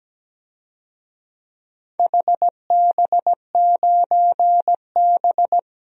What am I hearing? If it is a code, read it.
Morse code, heard as a keyed tone, about 17 wpm, 700 Hz. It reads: HB9B
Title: DXCC-100 Call Signs - 4 Characters - 17wpm